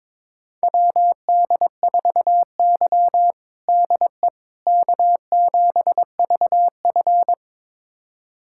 Morse code, heard as a keyed tone, about 22 wpm, 700 Hz. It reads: WD4Y DE K7VF
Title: USA Callsign DE Callsign - 4 Characters - 22wpm